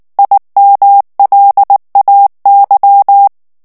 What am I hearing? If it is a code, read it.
Morse code, heard as a keyed tone, about 19 wpm, 800 Hz. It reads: IMLAY
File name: ANGLAIS texte clair 3 vitesse 18 mots minute